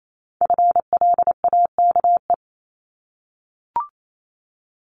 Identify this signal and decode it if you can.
Morse code, heard as a keyed tone, about 28 wpm, 700 Hz. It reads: FLAKE
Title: Multi-letter Phonemes 28wpm